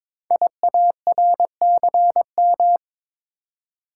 Morse code, heard as a keyed tone, about 22 wpm, 700 Hz. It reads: IARCM
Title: Code Groups - 5 Letters - 22wpm